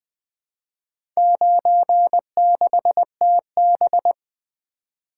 Morse code, heard as a keyed tone, about 20 wpm, 700 Hz. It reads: 96TB
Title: Four Letter-Number Combinations - 20wpm